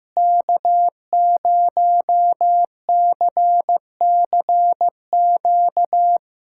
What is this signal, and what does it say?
Morse code, heard as a keyed tone, about 15 wpm, 700 Hz. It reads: K0CCQ
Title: USA Callsign DE Callsign - 5 Characters - 15wpm